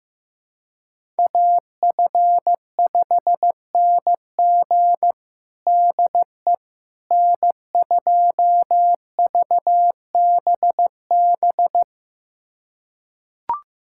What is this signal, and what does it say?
Morse code, heard as a keyed tone, about 15 wpm, 700 Hz. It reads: AF5NG DE N2VBB
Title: USA Callsign DE Callsign - 5 Characters - 15wpm